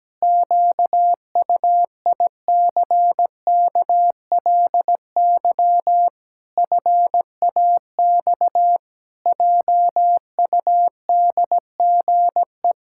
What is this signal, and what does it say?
Morse code, heard as a keyed tone, about 17 wpm, 700 Hz. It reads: QUICKLY FAX JUDGE
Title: Pangram Challenge v2 - 17wpm